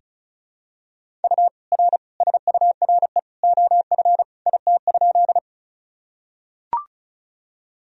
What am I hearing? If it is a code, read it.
Morse code, heard as a keyed tone, about 36 wpm, 700 Hz. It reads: U R SURE OF IT?
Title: Sentences from Top 300 Words and Abbreviations 35wpm